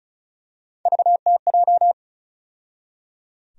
Morse code, heard as a keyed tone, about 34 wpm, 700 Hz. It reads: VTJ
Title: USA Call Sign Postfixes - 3 Letters 35wpm